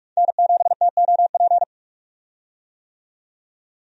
Morse code, heard as a keyed tone, about 46 wpm, 700 Hz. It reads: N7TOP
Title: USA Call Signs - 5 Characters - 45wpm